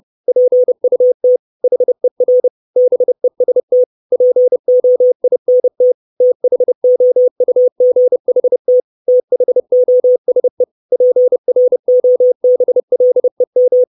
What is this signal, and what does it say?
Morse code, heard as a keyed tone, about 30 wpm, 500 Hz. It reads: PUT HER BEST POINT THOUGHT THOSE PROBLEM
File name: 632 Words 30WPM